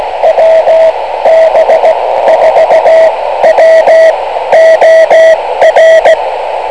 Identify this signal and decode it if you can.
Morse code, heard as a keyed tone, about 16 wpm, spaced as for 13 wpm, 650 Hz, with radio noise. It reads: WB4WOR
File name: kl7fdq / beacons / wb4wor-rst429